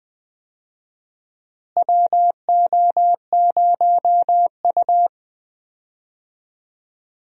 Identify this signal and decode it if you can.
Morse code, heard as a keyed tone, about 19 wpm, 700 Hz. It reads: WO0U
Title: USA Call Signs - 4 Characters - Repeated 3x - 20wpm